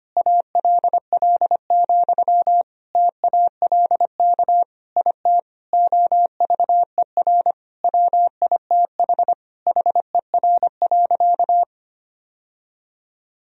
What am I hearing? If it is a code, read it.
Morse code, heard as a keyed tone, about 25 wpm, 700 Hz. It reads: ALL, TALK IT OVER WITH HER.
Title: Sentences from Top 1000 Words - Review Part 2 - 25wpm